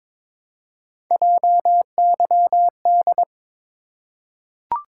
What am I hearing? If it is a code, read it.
Morse code, heard as a keyed tone, about 22 wpm, 700 Hz. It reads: JYD
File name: practice / 22wpm / Three Letter-Number Combinations - 22wpm